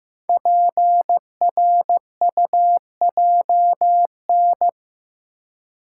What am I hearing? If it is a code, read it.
Morse code, heard as a keyed tone, about 15 wpm, 700 Hz. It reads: PRUJN
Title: Code Groups - 5 Letters - 15wpm